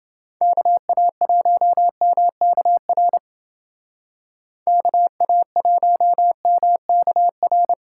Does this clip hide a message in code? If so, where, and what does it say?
Yes: Morse code from 0.4 to 7.7 s, heard as a keyed tone, about 31 wpm, 700 Hz. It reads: KA1MKR KA1MKR